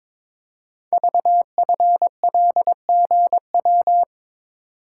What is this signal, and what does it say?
Morse code, heard as a keyed tone, about 22 wpm, 700 Hz. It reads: VFLGW
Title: Code Groups - 5 Letters - 22wpm